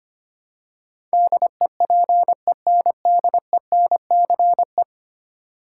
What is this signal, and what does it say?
Morse code, heard as a keyed tone, about 25 wpm, 700 Hz. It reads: DEPENDENCE
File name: 10 Letter Words - First 3 Letters Given - 25wpm